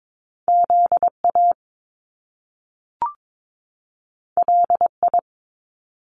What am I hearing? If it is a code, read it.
Morse code, heard as a keyed tone, about 22 wpm, 700 Hz. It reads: ZA LI
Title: Top Two Letter Combinations in Words 22wpm